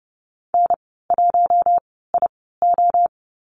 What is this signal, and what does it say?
Morse code, heard as a keyed tone, about 30 wpm, spaced as for 17 wpm, 700 Hz. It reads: N1IO
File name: Lesson 19g - Letter w - Call Signs 30wpm